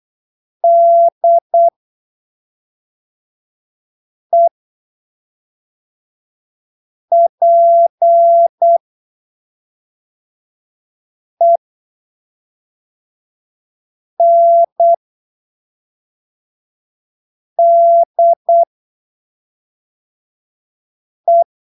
Morse code, heard as a keyed tone, about 8 wpm, spaced as for 3 wpm, 680 Hz. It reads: DEPENDE
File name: arquivos / texto / texto 22 08ppm